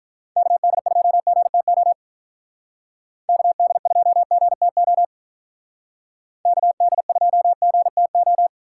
Morse code, heard as a keyed tone, about 42 wpm, 700 Hz. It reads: KD2GTO KD2GTO KD2GTO
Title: Speed Racing - USA 2x3 Call Signs 35wpm